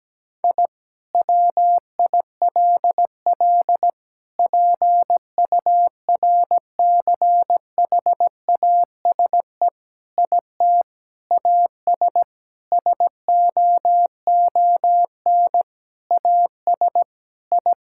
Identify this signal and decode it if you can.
Morse code, heard as a keyed tone, about 17 wpm, 700 Hz. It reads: I WILL PURCHASE IT AS SOON AS I
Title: Sentences from Top 1000 Words - Review Part 5 - 17wpm